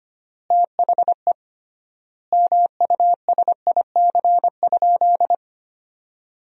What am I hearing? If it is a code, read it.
Morse code, heard as a keyed tone, about 25 wpm, 700 Hz. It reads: THE MUSIC?
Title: 3x Word-Spacing - Sentences from Top 300 Words - 25wpm